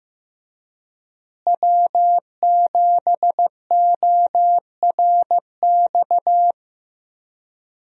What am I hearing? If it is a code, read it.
Morse code, heard as a keyed tone, about 15 wpm, 700 Hz. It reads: W7ORX
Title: USA Call Signs - 5 Characters - Repeated 2x - 15wpm